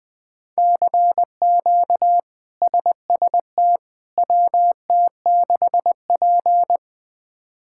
Morse code, heard as a keyed tone, about 20 wpm, 700 Hz. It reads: CQ SST WT6P
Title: K1USN SST QSO Practice - 20wpm